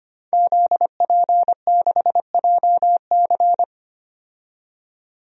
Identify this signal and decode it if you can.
Morse code, heard as a keyed tone, about 25 wpm, 700 Hz. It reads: ZP6JC
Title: DXCC-100 Call Signs - 5 Characters - 25wpm